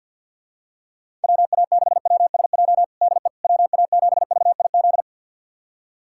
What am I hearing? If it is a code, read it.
Morse code, heard as a keyed tone, about 52 wpm, 700 Hz. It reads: WA6WSJ DE WA7VIZ
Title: USA Callsign DE Callsign - 6 Characters - 50wpm